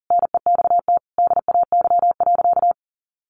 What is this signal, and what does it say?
Morse code, heard as a keyed tone, about 40 wpm, 700 Hz. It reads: NEXT DAY.
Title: Sentences from Top 1400 Words 40wpm